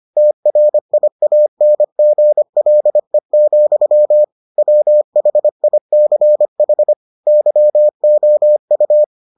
Morse code, heard as a keyed tone, about 25 wpm, 600 Hz. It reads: TRIANGLE, WHICH YOU